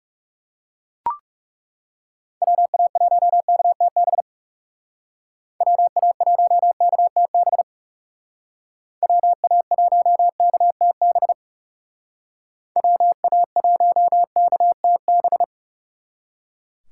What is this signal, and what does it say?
Morse code, heard as a keyed tone, about 38 wpm, 700 Hz. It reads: WA1KTB WA1KTB WA1KTB WA1KTB
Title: Speed Racing - USA 2x3 Call Signs 30wpm